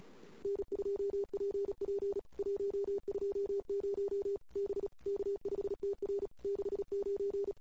Morse code, heard as a keyed tone, about 36 wpm, 400 Hz. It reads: N2PP 120 B K5TR 69
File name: CW / 2008-ss-cw-k5tr-op-n5rz -22:00